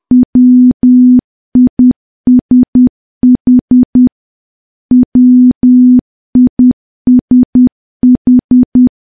Morse code, heard as a keyed tone, about 10 wpm, 250 Hz. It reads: WISH WISH
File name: morse / morse-training-10WPM